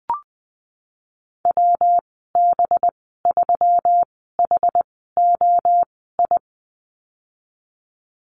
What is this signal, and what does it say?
Morse code, heard as a keyed tone, about 19 wpm, spaced as for 14 wpm, 700 Hz. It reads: WB3HOI
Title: 186 Lesson 30g - Letter b - Call Signs 20wpm